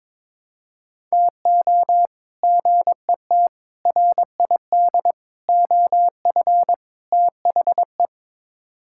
Morse code, heard as a keyed tone, about 21 wpm, 700 Hz. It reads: TO GET RID OF THE
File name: practice / 22wpm / Build A Phrase - 5-Word Phrases - 22wpm